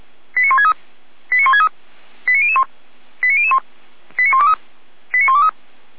Five-tone selective call (ZVEI), heard as quick runs of five tones, tones of about 70 ms. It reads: 89163 89163 89001 89001 89123 89123
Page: SDR/ZVEI